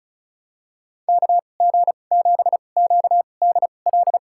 Morse code, heard as a keyed tone, about 36 wpm, spaced as for 25 wpm, 700 Hz. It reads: KG7QDL
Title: USA Call Signs - 6 Characters - 2x Character Spacing - 35wpm